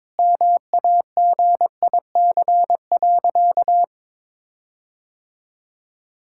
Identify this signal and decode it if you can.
Morse code, heard as a keyed tone, about 22 wpm, 700 Hz. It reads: MAGIC.
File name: practice / 22wpm / Sentences with 4-Word Phrases - 22wpm